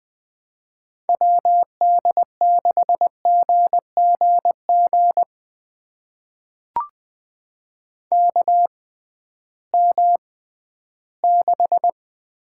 Morse code, heard as a keyed tone, about 21 wpm, spaced as for 13 wpm, 700 Hz. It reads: WD6GGG KM6
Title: USA Call Signs - 6 Characters - 6x Character Spacing - 20wpm